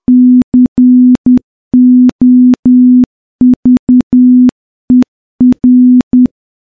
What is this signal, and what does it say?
Morse code, heard as a keyed tone, about 10 wpm, 250 Hz. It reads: COVER
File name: morse-training-10WPM